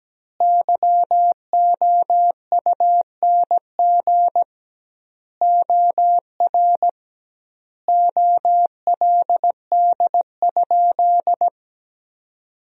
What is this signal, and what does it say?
Morse code, heard as a keyed tone, about 17 wpm, 700 Hz. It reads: YOUNG OR OLD?
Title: 2x Word-Spacing - Sentences from Top 400 Words - 17wpm